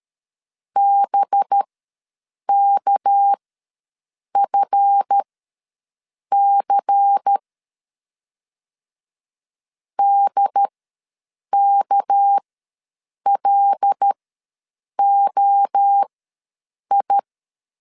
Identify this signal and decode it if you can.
Morse code, heard as a keyed tone, about 13 wpm, spaced as for 7 wpm, 800 Hz. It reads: BKFC DKLOI
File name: LEZIONI / 7 GF9Z / 7 GF9Z ws 12wpm 3321